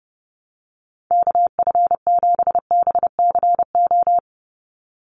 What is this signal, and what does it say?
Morse code, heard as a keyed tone, about 30 wpm, 700 Hz. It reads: KF7BCO
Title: USA Call Signs 30wpm